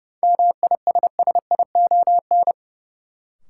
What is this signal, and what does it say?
Morse code, heard as a keyed tone, about 30 wpm, 700 Hz. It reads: MISSION
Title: Sentences from Top 1600 Words - New Word Drill - 30wpm